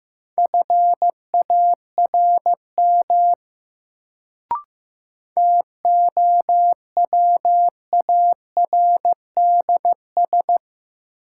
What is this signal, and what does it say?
Morse code, heard as a keyed tone, about 15 wpm, 700 Hz. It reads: FARM TOWARDS
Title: Sentences from Top 1000 Words - Review Part 2 - New Word Drill - 15wpm